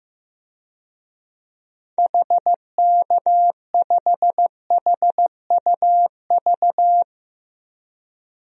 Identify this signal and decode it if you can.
Morse code, heard as a keyed tone, about 15 wpm, 700 Hz. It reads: HK5HUV
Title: DXCC-100 Call Signs - 6 Characters - 15wpm